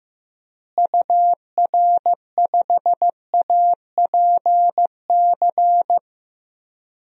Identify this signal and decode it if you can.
Morse code, heard as a keyed tone, about 15 wpm, 700 Hz. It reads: UR5APC
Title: DXCC-100 Call Signs - 6 Characters - Repeated 2x - 15wpm